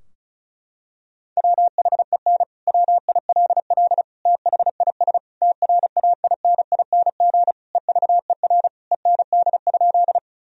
Morse code, heard as a keyed tone, about 36 wpm, 700 Hz. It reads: WHEN WILL THIS TRAINING EVER END?
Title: Sentences from Top 1000 Words - Review Part 4 - 35wpm